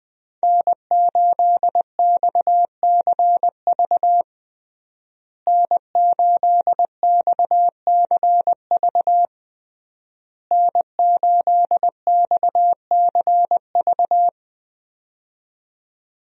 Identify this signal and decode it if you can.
Morse code, heard as a keyed tone, about 20 wpm, 700 Hz. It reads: N8XCV N8XCV N8XCV
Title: USA Call Signs - 5 Characters - Repeated 3x - 20wpm